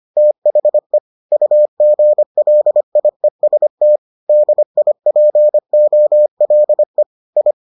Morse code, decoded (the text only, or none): THE UGLIEST DIPOLE I